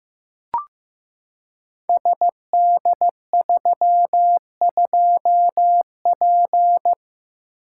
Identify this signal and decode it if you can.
Morse code, heard as a keyed tone, about 15 wpm, 700 Hz. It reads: SD32P
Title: Code Groups - 5 Letter:Numbers - 15wpm